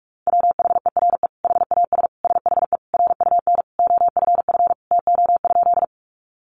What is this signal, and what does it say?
Morse code, heard as a keyed tone, about 44 wpm, 700 Hz. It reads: WHERE HAS SHE RUN OFF TO?
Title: Sentences from Top 300 Words 45wpm